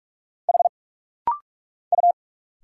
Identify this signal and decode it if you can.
Morse code, heard as a keyed tone, about 46 wpm, 700 Hz. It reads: H U
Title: Single Letters - Rapid-Fire 45wpm